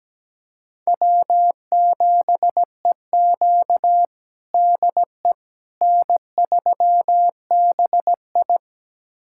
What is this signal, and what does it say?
Morse code, heard as a keyed tone, about 17 wpm, 700 Hz. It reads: W7EQ DE N3BI
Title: USA Callsign DE Callsign - 4 Characters - 17wpm